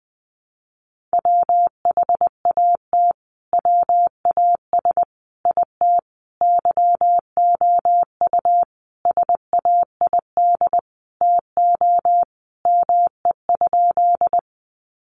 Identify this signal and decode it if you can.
Morse code, heard as a keyed tone, about 20 wpm, 700 Hz. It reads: WHAT WAS IT YOU SAID TO ME?
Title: Sentences from Top 200 Words 20wpm